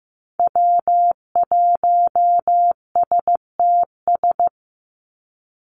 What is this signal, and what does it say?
Morse code, heard as a keyed tone, about 15 wpm, 700 Hz. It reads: W1STS
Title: USA Call Signs 15wpm